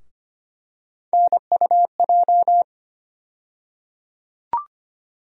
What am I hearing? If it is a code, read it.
Morse code, heard as a keyed tone, about 25 wpm, 700 Hz. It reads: NUJ
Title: USA Call Sign Postfixes - 3 Letters 25wpm